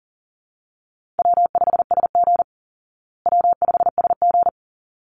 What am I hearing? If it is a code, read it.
Morse code, heard as a keyed tone, about 40 wpm, 700 Hz. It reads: W5SG W5SG
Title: Call Signs Repeated 3x - 40wpm